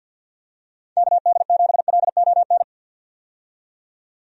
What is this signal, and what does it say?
Morse code, heard as a keyed tone, about 53 wpm, 700 Hz. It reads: KD7LON
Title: USA Call Signs - 6 Characters - 50wpm